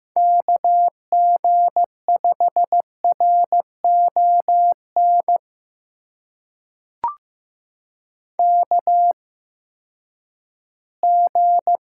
Morse code, heard as a keyed tone, about 15 wpm, spaced as for 10 wpm, 700 Hz. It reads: KG5RON KG
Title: USA Call Signs - 6 Characters - 8x Character Spacing - 15wpm